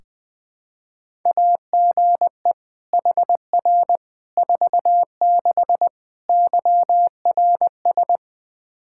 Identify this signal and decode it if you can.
Morse code, heard as a keyed tone, about 20 wpm, 700 Hz. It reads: AGE HR 46 YRS
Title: Practice QSO - Exchange 3 - More Info - 20wpm